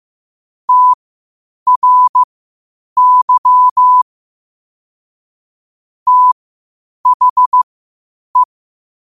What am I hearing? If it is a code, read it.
Morse code, heard as a keyed tone, about 15 wpm, spaced as for 8 wpm, 1000 Hz. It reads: TRY THE
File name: qotdfiles / 8wpm